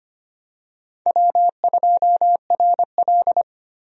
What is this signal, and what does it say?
Morse code, heard as a keyed tone, about 25 wpm, 700 Hz. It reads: W2RL
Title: USA Call Signs - 4 Characters - 25wpm